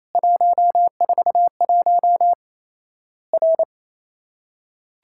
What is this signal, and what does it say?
Morse code, heard as a keyed tone, about 28 wpm, 700 Hz. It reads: 141 R
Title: Worked All Europe DX Contest - QTC Exchange - 28wpm